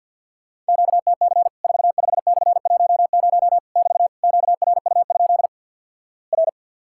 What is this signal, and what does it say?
Morse code, heard as a keyed tone, about 50 wpm, 700 Hz. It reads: QTC 45/10 <BT> QRU? R
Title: Worked All Europe DX Contest - QTC Exchange with Repeats - 50wpm